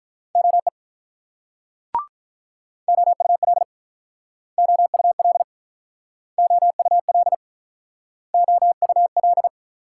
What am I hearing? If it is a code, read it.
Morse code, heard as a keyed tone, about 48 wpm, 700 Hz. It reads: OE OUL OUL OUL OUL